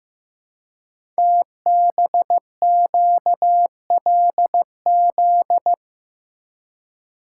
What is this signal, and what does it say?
Morse code, heard as a keyed tone, about 15 wpm, 700 Hz. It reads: TBQLZ
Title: Code Groups - 5 Letter:Numbers - 15wpm